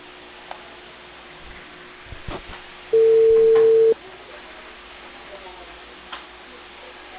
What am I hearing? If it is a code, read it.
Ringback tone (China, 450 Hz), with speech.